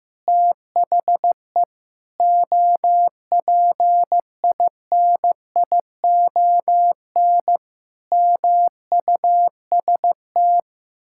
Morse code, heard as a keyed tone, about 15 wpm, 700 Hz. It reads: THE OPINION MUST
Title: Sentences from Top 1600 Words 15wpm